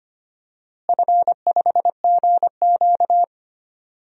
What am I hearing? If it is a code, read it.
Morse code, heard as a keyed tone, about 25 wpm, 700 Hz. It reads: F5GQ
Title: DXCC-100 Call Signs - 4 Characters - Repeated 3x - 25wpm